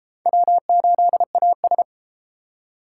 Morse code, heard as a keyed tone, about 32 wpm, 700 Hz. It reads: W8AS